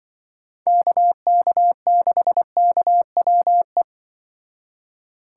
Morse code, heard as a keyed tone, about 24 wpm, 700 Hz. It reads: KK6KWE
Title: Speed Racing - USA 2x3 Call Signs 20wpm